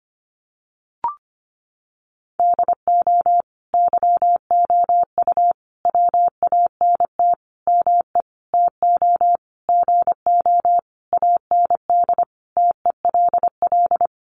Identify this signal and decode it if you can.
Morse code, heard as a keyed tone, about 25 wpm, 700 Hz. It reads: DO YOU WANT ME TO GO AND TELL
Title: Sentences from Top 200 Words 25wpm